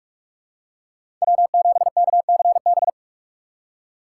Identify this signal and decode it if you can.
Morse code, heard as a keyed tone, about 46 wpm, 700 Hz. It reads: W7KCB
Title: USA Call Signs - 6x Character Spacing - 45wpm